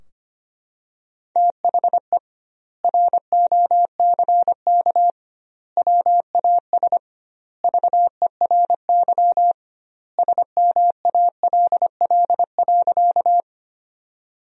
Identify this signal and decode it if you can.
Morse code, heard as a keyed tone, about 25 wpm, 700 Hz. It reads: THE ROCK WAS VERY SMALL.
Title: 2x Word-Spacing - Sentences from Top 400 Words - 25wpm